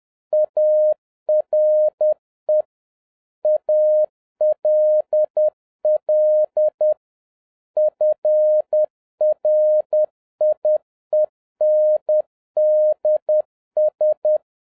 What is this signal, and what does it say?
Morse code, heard as a keyed tone, about 10 wpm, 610 Hz. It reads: ARE ALL FRIENDS
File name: 2011-07-mp3 / n46 JOHN K3TN 10wpm